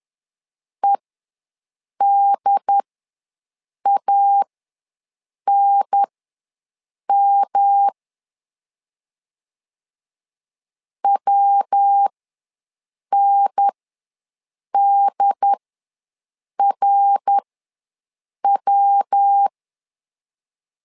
Morse code, heard as a keyed tone, about 10 wpm, spaced as for 6 wpm, 800 Hz. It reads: EDANM WNDRW
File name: LEZIONI / 3 MWRO / 3 MWR ws 10wpm 2151